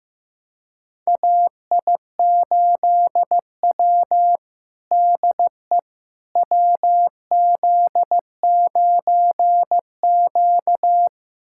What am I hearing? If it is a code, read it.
Morse code, heard as a keyed tone, about 16 wpm, 700 Hz. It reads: AI8W DE WZ9Q
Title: USA Callsign DE Callsign - 4 Characters - 15wpm